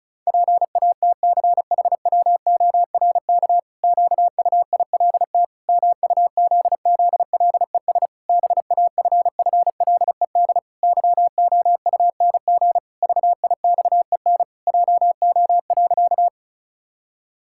Morse code, heard as a keyed tone, about 36 wpm, 700 Hz. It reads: PATCHWORK QUILT MUZZLES BAFFLED YOUNG VIXEN JO.
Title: Pangram Challenge v2 - 35wpm